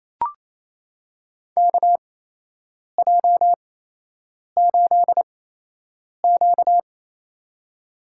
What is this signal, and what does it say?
Morse code, heard as a keyed tone, about 28 wpm, spaced as for 8 wpm, 700 Hz. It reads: KJ8Q
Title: USA Call Signs - 4 Characters - 8x Character Spacing - 28wpm